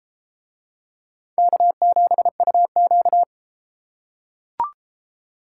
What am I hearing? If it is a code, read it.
Morse code, heard as a keyed tone, about 34 wpm, 700 Hz. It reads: K7UQ